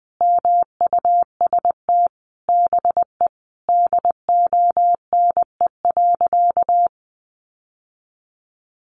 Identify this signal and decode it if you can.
Morse code, heard as a keyed tone, about 20 wpm, 700 Hz. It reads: MUST BE DONE.
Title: Sentences from Top 1100 Words 20wpm